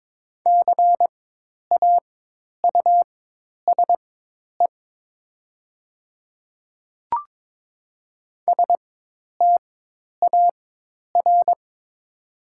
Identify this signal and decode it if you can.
Morse code, heard as a keyed tone, about 22 wpm, spaced as for 10 wpm, 700 Hz. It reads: CAUSE STAR
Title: 4x Character Spacing - Top 500 Words - 22wpm